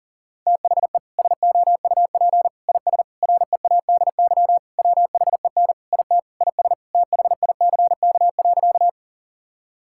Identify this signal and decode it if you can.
Morse code, heard as a keyed tone, about 40 wpm, 700 Hz. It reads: THE SOUP IS READY WHEN IT IS THICK.
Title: Sentences from Top 1600 Words 40wpm